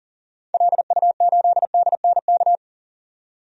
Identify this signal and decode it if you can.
Morse code, heard as a keyed tone, about 40 wpm, 700 Hz. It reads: LU8DNK
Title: DXCC-100 Call Signs - 6 Characters - 40wpm